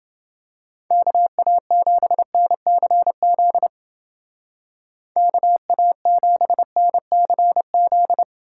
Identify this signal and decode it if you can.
Morse code, heard as a keyed tone, about 30 wpm, 700 Hz. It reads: KA7NCZ KA7NCZ